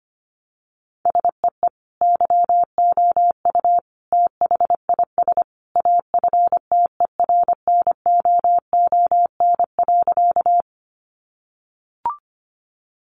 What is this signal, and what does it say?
Morse code, heard as a keyed tone, about 25 wpm, 700 Hz. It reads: SEE YOU THIS AFTERNOON.
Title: Sentences from Top 1500 Words 25wpm